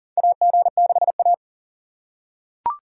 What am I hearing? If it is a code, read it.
Morse code, heard as a keyed tone, about 40 wpm, 700 Hz. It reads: AG6A
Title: USA Call Signs - 4 Characters - 4x Character Spacing - 40wpm